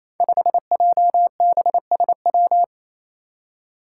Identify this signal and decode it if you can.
Morse code, heard as a keyed tone, about 27 wpm, 700 Hz. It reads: 5JBSW
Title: Five Challenging Letter-Number Combinations 28wpm